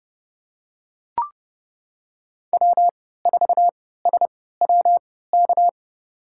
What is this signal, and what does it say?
Morse code, heard as a keyed tone, about 32 wpm, spaced as for 17 wpm, 700 Hz. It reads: W4SWK
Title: USA Call Signs - 3x Character Spacing - 30wpm